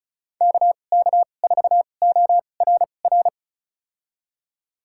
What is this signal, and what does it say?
Morse code, heard as a keyed tone, about 36 wpm, spaced as for 25 wpm, 700 Hz. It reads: KK4ORR
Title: USA Call Signs - 2x Character Spacing - 35wpm